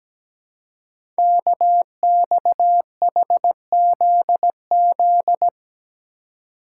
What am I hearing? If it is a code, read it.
Morse code, heard as a keyed tone, about 17 wpm, 700 Hz. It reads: KXHZZ
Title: Code Groups - 5 Letters - 17wpm